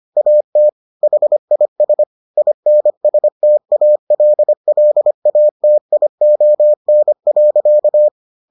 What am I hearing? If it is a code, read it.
Morse code, heard as a keyed tone, about 25 wpm, 600 Hz. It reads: AT HIS INSTALLATION.